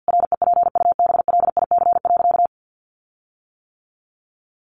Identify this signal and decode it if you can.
Morse code, heard as a keyed tone, about 49 wpm, 700 Hz. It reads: REPUBLIC.
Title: Sentences from Top 1500 Words 50wpm